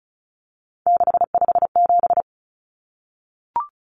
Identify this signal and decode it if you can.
Morse code, heard as a keyed tone, about 34 wpm, 700 Hz. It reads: 657
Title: Numbers - 3-Digit 35wpm